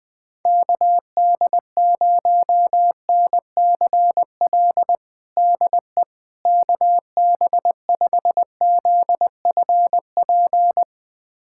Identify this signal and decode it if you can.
Morse code, heard as a keyed tone, about 20 wpm, 700 Hz. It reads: KD0NCL DE KB5ZFP
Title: USA Callsign DE Callsign - 6 Characters - 20wpm